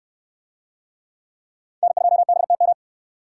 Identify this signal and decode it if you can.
Morse code, heard as a keyed tone, about 63 wpm, 700 Hz. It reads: N2BTM